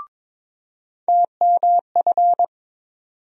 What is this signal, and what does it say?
Morse code, heard as a keyed tone, about 22 wpm, 700 Hz. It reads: TMF
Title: USA Call Sign Postfixes - 3 Letters 22wpm